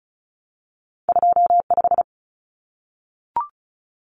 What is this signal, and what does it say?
Morse code, heard as a keyed tone, about 36 wpm, 700 Hz. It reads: 25
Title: Common Cut Numbers - 2-Digit 35wpm